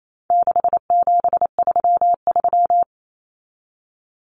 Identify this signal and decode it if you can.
Morse code, heard as a keyed tone, about 28 wpm, 700 Hz. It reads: 6733
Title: Numbers - 4-Digit 28wpm